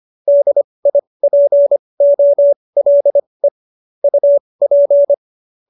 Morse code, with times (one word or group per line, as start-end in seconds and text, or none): 0.3-3.5 DIPOLE
4.0-5.1 UP